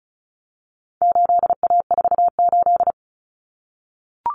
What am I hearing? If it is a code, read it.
Morse code, heard as a keyed tone, about 34 wpm, 700 Hz. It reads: 8A48
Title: Common Cut Numbers - 4-Digit 35wpm